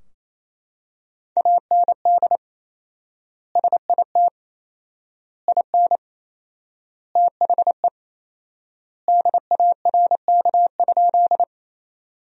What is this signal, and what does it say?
Morse code, heard as a keyed tone, about 28 wpm, 700 Hz. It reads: AND SIT IN THE DARK?
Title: 4x Word-Spacing - Sentences from Top 500 Words - 28wpm